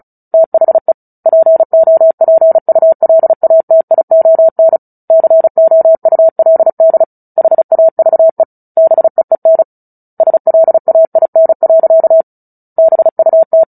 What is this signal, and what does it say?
Morse code, heard as a keyed tone, about 34 wpm, 650 Hz. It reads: THE POPULATION COULD HAVE BEEN SLAIN. BUT
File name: Book Two / 09-Chapter 9 - 35 WPM